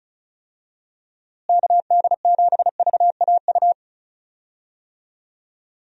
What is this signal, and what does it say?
Morse code, heard as a keyed tone, about 36 wpm, 700 Hz. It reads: KD7VAU
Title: USA Call Signs - 6 Characters - Repeated 2x - 35wpm